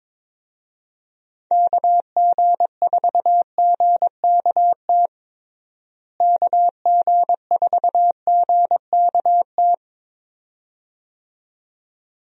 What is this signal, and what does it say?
Morse code, heard as a keyed tone, about 22 wpm, 700 Hz. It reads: KG4GKT KG4GKT
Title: USA Call Signs - 6 Characters - Repeated 2x - 22wpm